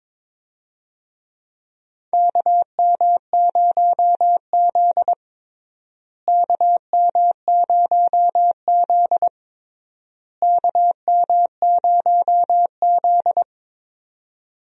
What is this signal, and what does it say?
Morse code, heard as a keyed tone, about 22 wpm, 700 Hz. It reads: KM0Z KM0Z KM0Z
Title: USA Call Signs - 4 Characters - Repeated 3x - 22wpm